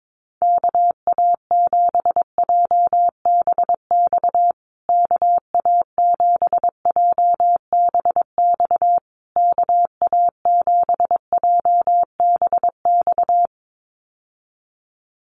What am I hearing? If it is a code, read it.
Morse code, heard as a keyed tone, about 22 wpm, 700 Hz. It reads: KA7JBX KA7JBX KA7JBX
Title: USA Call Signs Repeated 3x 22wpm